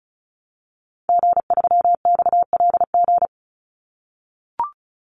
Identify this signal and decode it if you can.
Morse code, heard as a keyed tone, about 34 wpm, 700 Hz. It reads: G3XLG
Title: Call Signs - 35wpm